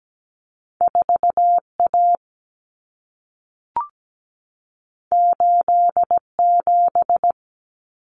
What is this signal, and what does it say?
Morse code, heard as a keyed tone, about 17 wpm, 700 Hz. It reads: 4A 87
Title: Common Cut Numbers - 2-Digit 17wpm